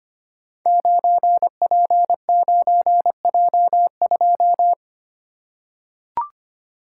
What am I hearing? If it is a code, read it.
Morse code, heard as a keyed tone, about 25 wpm, 700 Hz. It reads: 9P9J2
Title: Code Groups - 5 Letter:Numbers - 25wpm